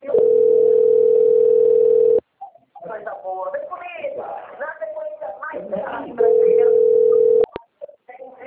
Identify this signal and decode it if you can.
Ringback tone (North America), 440 + 480 Hz, with speech.
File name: rg-1600-unknown- - - .26292